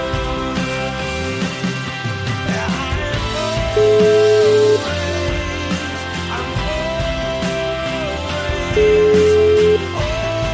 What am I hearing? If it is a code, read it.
Ringback tone (Europe, 425 Hz), with noise.